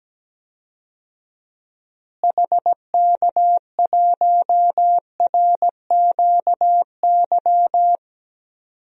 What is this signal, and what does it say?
Morse code, heard as a keyed tone, about 17 wpm, 700 Hz. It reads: HK1RQY
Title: DXCC-100 Call Signs - 6 Characters - Repeated 2x - 17wpm